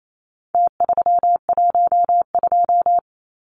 Morse code, heard as a keyed tone, about 28 wpm, 700 Hz. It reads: T312
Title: Numbers - Contest Style - 4-Digit 28wpm